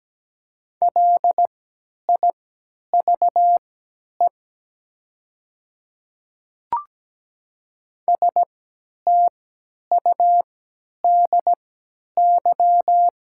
Morse code, heard as a keyed tone, about 17 wpm, spaced as for 10 wpm, 700 Hz. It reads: LIVE STUDY
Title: 3x Character Spacing - Top 500 Words - 17wpm